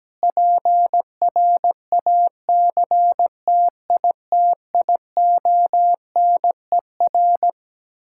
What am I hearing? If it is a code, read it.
Morse code, heard as a keyed tone, about 17 wpm, 700 Hz. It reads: PRACTITIONER
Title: Practice QSO - Exchange 3 - More Info - 17wpm